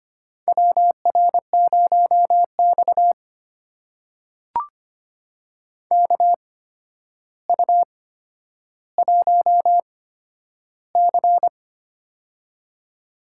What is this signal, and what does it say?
Morse code, heard as a keyed tone, about 25 wpm, spaced as for 11 wpm, 700 Hz. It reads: WR0X KU1C
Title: USA Call Signs - 4 Characters - 8x Character Spacing - 25wpm